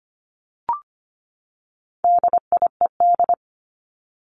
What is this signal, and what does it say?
Morse code, heard as a keyed tone, about 25 wpm, 700 Hz. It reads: DIED
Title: ED Suffix - Top 100 Words with Base Words 25wpm